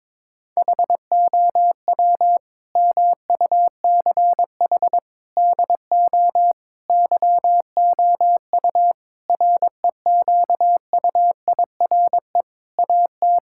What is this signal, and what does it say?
Morse code, heard as a keyed tone, about 22 wpm, 700 Hz. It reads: HOW MUCH DO YOU REQUIRE AT
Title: Sentences from Top 1000 Words - Review Part 6 - 22wpm